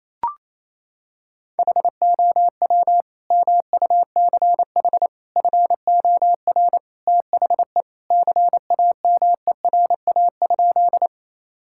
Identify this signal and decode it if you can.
Morse code, heard as a keyed tone, about 28 wpm, 700 Hz. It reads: HOW MUCH FOR THE CAMERA?
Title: Sentences from Top 1000 Words - Review Part 5 - 28wpm